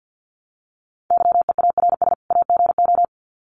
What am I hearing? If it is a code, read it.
Morse code, heard as a keyed tone, about 50 wpm, 700 Hz. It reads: YEARS AGO
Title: Sets of 2 Words 50wpm